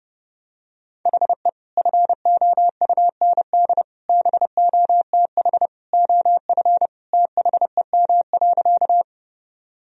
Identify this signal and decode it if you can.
Morse code, heard as a keyed tone, about 30 wpm, 700 Hz. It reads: HE FOUND BOTH OF THEM.
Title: Sentences from Top 1000 Words - Review Part 1 - 30wpm